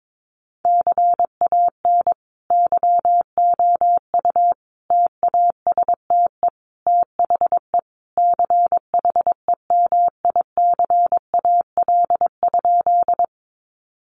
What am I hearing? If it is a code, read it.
Morse code, heard as a keyed tone, about 22 wpm, 700 Hz. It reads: CAN YOU TASTE THE CHEMICAL?
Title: Sentences from Top 1200 Words 22wpm